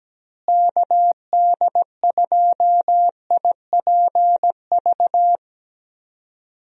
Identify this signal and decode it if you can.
Morse code, heard as a keyed tone, about 17 wpm, 700 Hz. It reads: KD2IPV
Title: USA Call Signs - 6 Characters - Repeated 3x - 17wpm